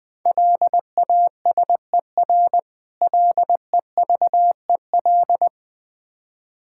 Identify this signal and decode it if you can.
Morse code, heard as a keyed tone, about 20 wpm, 690 Hz. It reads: LASER LEVEL
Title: Difficult Words with Hints - 20wpm